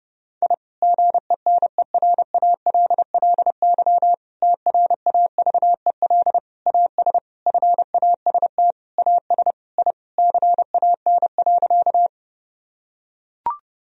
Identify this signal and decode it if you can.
Morse code, heard as a keyed tone, about 30 wpm, 700 Hz. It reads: I GENERALLY TRAVEL AS FAST AS I CAN.
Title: Sentences from Top 1000 Words - Review Part 4 - 30wpm